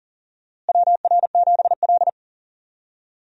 Morse code, heard as a keyed tone, about 40 wpm, 700 Hz. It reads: WR7L
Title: USA Call Signs - 4 Characters - Repeated 3x - 40wpm